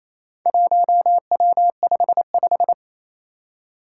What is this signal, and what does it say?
Morse code, heard as a keyed tone, about 28 wpm, 700 Hz. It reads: 1W55
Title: Four Challenging Letter-Number Combinations 28wpm